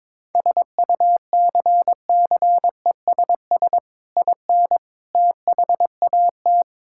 Morse code, heard as a keyed tone, about 22 wpm, 700 Hz. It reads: SUCCESS IN THAT
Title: Sentences from Top 1000 Words - Review Part 5 - 22wpm